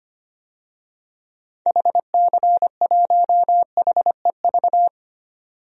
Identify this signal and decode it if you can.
Morse code, heard as a keyed tone, about 25 wpm, 700 Hz. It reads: HC1HEV
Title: DXCC-100 Call Signs - 6 Characters - 25wpm